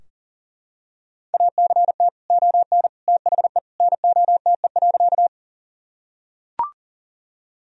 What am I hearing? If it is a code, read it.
Morse code, heard as a keyed tone, about 37 wpm, 700 Hz. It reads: ACT ON THE NOTE.
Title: Sentences from Top 1000 Words - Review Part 2 - 40wpm